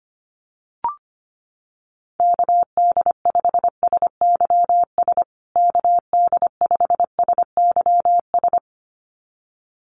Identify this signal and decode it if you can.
Morse code, heard as a keyed tone, about 25 wpm, 700 Hz. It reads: KD5SYS KD5SYS
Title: USA Call Signs Repeated 2x 25wpm